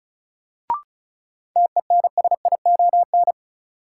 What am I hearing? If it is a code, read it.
Morse code, heard as a keyed tone, about 36 wpm, 700 Hz. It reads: TENSION
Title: Sentences from Top 1600 Words - New Word Drill - 35wpm